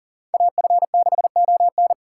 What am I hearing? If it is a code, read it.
Morse code, heard as a keyed tone, about 40 wpm, 700 Hz. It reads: AF6ON
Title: USA Call Signs - 2x Character Spacing - 40wpm